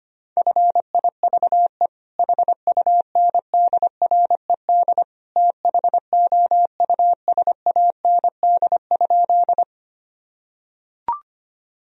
Morse code, heard as a keyed tone, about 25 wpm, 700 Hz. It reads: FIVE HUNDRED THOUSAND?
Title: Sentences from Top 1000 Words - Review Part 3 - 25wpm